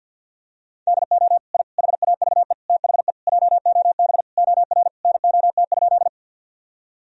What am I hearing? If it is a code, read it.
Morse code, heard as a keyed tone, about 52 wpm, 700 Hz. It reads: DO I HAVE THE JOB OR NOT?
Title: Sentences from Top 1000 Words - Review Part 3 - 50wpm